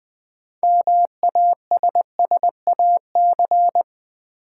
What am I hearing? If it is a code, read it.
Morse code, heard as a keyed tone, about 20 wpm, 700 Hz. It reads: MASSAC
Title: Build A Word - 3 Letters At A Time - 8-Letter Words - 20wpm